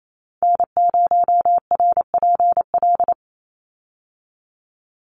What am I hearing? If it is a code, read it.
Morse code, heard as a keyed tone, about 28 wpm, 700 Hz. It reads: N0RPL
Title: USA Call Signs 28wpm